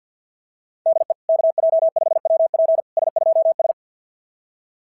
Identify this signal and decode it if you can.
Morse code, heard as a keyed tone, about 52 wpm, 690 Hz. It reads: DE KJ5WP S2S
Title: SOTA QSO Practice 50wpm